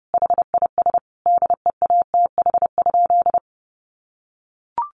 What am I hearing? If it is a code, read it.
Morse code, heard as a keyed tone, about 30 wpm, 700 Hz. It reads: HIS DEATH?
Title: Sentences from Top 1100 Words 30wpm